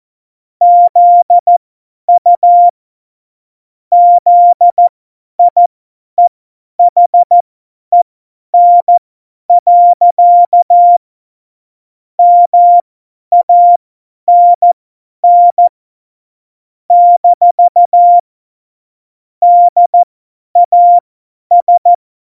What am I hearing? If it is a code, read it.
Morse code, heard as a keyed tone, about 14 wpm, spaced as for 10 wpm, 700 Hz. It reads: ZU ZIEHEN. MANN - DAS
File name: Morsekurs / Witze / Witz 11 Juli 17, Tempo 50 F70